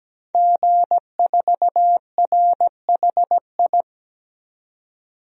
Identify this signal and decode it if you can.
Morse code, heard as a keyed tone, about 17 wpm, 700 Hz. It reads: G4RHI
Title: DXCC-100 Call Signs - 5 Characters - 17wpm